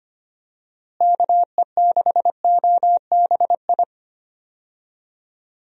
Morse code, heard as a keyed tone, about 25 wpm, 700 Hz. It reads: KE6OBI
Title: USA Call Signs - 6 Characters - Repeated 3x - 25wpm